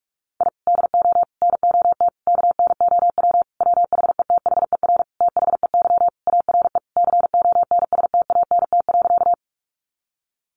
Morse code, heard as a keyed tone, about 44 wpm, 700 Hz. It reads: I DO NOT KNOW WHETHER THEY ARE CONSTANT.
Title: Sentences from Top 1500 Words 45wpm